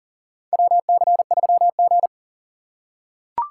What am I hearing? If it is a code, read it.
Morse code, heard as a keyed tone, about 40 wpm, 700 Hz. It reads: WC3G
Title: USA Call Signs - 4 Characters - 3x Character Spacing - 40wpm